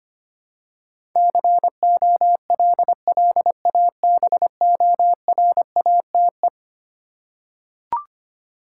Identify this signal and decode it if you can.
Morse code, heard as a keyed tone, about 25 wpm, 700 Hz. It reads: COLLABORATE
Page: morse-code-ninja-practice-25wpm